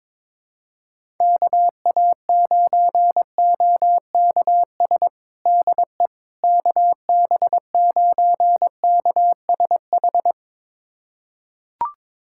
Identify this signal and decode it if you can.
Morse code, heard as a keyed tone, about 22 wpm, 700 Hz. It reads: KA9OKS DE KB9KSH
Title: USA Callsign DE Callsign - 6 Characters - 22wpm